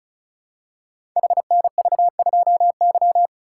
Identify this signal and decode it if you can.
Morse code, heard as a keyed tone, about 36 wpm, 700 Hz. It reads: HNV2Y
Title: Code Groups - 5 Letter:Numbers - 35wpm